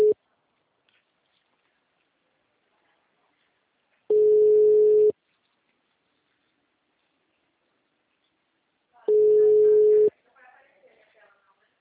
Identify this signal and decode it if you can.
Ringback tone (Europe, 425 Hz), with speech.